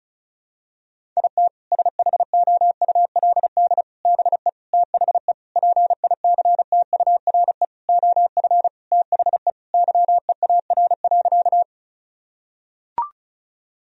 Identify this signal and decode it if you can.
Morse code, heard as a keyed tone, about 36 wpm, 700 Hz. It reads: IT SHOULD BE THE PICTURE OF THE YEAR.
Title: Sentences from Top 1000 Words - Review Part 1 - 35wpm